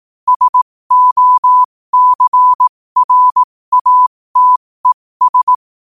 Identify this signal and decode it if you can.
Morse code, heard as a keyed tone, about 18 wpm, spaced as for 15 wpm, 1000 Hz. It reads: SOCRATES
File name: qotdfiles / 15wpm